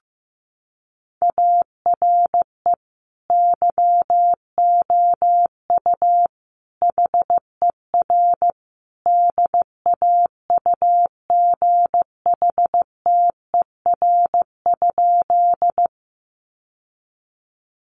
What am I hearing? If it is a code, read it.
Morse code, heard as a keyed tone, about 14 wpm, 700 Hz. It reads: ARE YOU HER DAUGHTER?
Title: Sentences from Top 1500 Words 15wpm